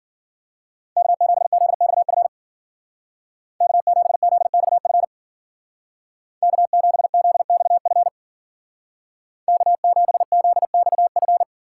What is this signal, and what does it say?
Morse code, heard as a keyed tone, about 52 wpm, 700 Hz. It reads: K7ZXF K7ZXF K7ZXF K7ZXF